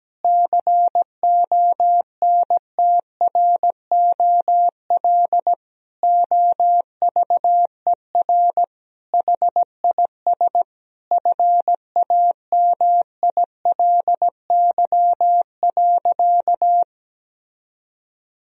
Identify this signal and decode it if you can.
Morse code, heard as a keyed tone, about 17 wpm, 700 Hz. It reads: CONTROL OVER HIS FAMILY.
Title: Sentences from Top 1000 Words - Review Part 3 - 17wpm